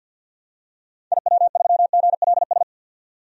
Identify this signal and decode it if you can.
Morse code, heard as a keyed tone, about 50 wpm, 700 Hz. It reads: IW3GLS
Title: DXCC-100 Call Signs - 6 Characters - 50wpm